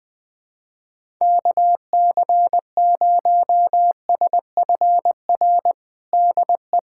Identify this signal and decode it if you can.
Morse code, heard as a keyed tone, about 21 wpm, 700 Hz. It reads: KC0SFR DE
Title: USA Callsign DE Callsign - 6 Characters - 20wpm